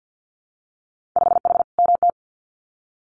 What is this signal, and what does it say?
Morse code, heard as a keyed tone, about 50 wpm, 700 Hz. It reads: 5H MT
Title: Winter Field Day Exchanges Encore 50wpm